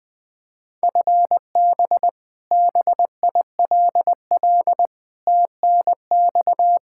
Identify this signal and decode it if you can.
Morse code, heard as a keyed tone, about 21 wpm, 700 Hz. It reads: FB BILL TNX
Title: Practice QSO - Exchange 3 - More Info - 20wpm